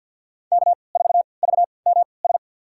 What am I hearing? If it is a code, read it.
Morse code, heard as a keyed tone, about 52 wpm, spaced as for 29 wpm, 700 Hz. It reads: K4VMS
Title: USA Call Signs - 5 Characters - 3x Character Spacing - 50wpm